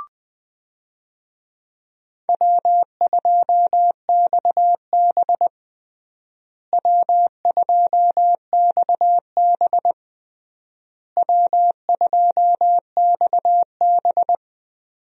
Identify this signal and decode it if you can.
Morse code, heard as a keyed tone, about 20 wpm, 700 Hz. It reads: W2XB W2XB W2XB
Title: USA Call Signs - 4 Characters - Repeated 3x - 20wpm